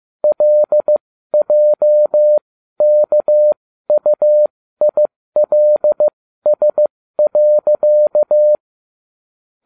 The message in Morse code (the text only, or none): LJKUILS.